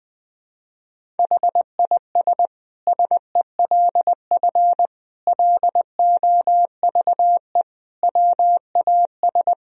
Morse code, heard as a keyed tone, about 21 wpm, 700 Hz. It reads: HIS SELF LOVE WAS
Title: Sentences from Top 1000 Words - Review Part 1 - 20wpm